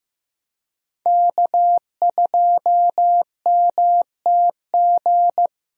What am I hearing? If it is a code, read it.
Morse code, heard as a keyed tone, about 15 wpm, 700 Hz. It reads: K2MTG
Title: USA Call Signs - 5 Characters - Repeated 3x - 15wpm